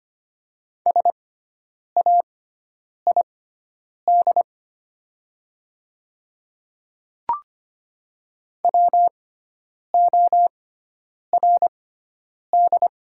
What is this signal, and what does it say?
Morse code, heard as a keyed tone, about 25 wpm, spaced as for 9 wpm, 700 Hz. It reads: SAID WORD
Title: 6x Character Spacing - Top 100 Words - 25wpm